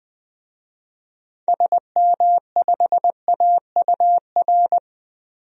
Morse code, heard as a keyed tone, about 20 wpm, 700 Hz. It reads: SM5AUR
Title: DXCC-100 Call Signs - 6 Characters - 20wpm